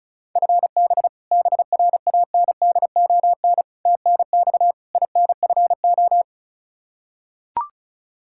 Morse code, heard as a keyed tone, about 36 wpm, 700 Hz. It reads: FB BRANDON TNX INFO
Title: Practice QSO - Exchange 3 - More Info - 35wpm